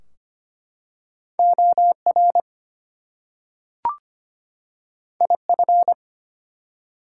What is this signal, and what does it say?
Morse code, heard as a keyed tone, about 25 wpm, 700 Hz. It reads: OR IF
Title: Two Letter Words - 25wpm